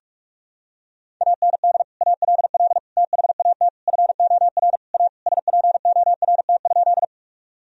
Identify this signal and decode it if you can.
Morse code, heard as a keyed tone, about 46 wpm, 700 Hz. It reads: AND ALL THAT FOR A SPORT?
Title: Sentences from Top 1000 Words - Review Part 4 - 45wpm